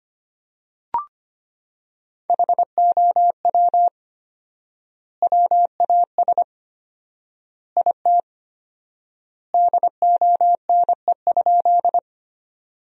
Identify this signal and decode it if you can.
Morse code, heard as a keyed tone, about 25 wpm, 700 Hz. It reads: HOW WAS IT DONE?
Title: 4x Word-Spacing - Sentences from Top 500 Words - 25wpm